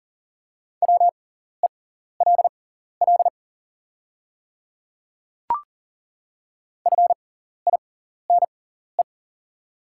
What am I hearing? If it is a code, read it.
Morse code, heard as a keyed tone, about 40 wpm, spaced as for 14 wpm, 700 Hz. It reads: WELL FINE
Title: 6x Character Spacing - Top 500 Words - 40wpm